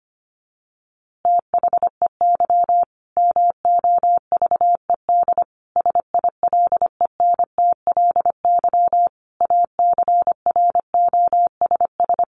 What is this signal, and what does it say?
Morse code, heard as a keyed tone, about 25 wpm, 700 Hz. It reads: THEY MOVED SILENTLY ACROSS
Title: A Sound Of Thunder 25wpm